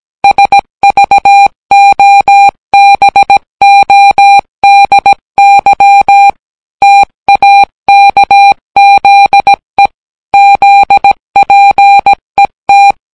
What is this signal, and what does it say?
Morse code, heard as a keyed tone, about 17 wpm, 800 Hz. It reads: SVOBODY TAKZE ZPET